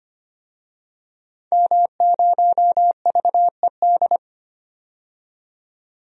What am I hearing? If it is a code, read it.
Morse code, heard as a keyed tone, about 25 wpm, 700 Hz. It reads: M0VED
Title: DXCC-100 Call Signs - 5 Characters - 25wpm